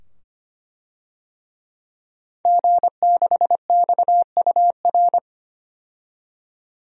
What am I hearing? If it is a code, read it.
Morse code, heard as a keyed tone, about 25 wpm, 700 Hz. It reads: G6XUR
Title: DXCC-100 Call Signs - 5 Characters - 25wpm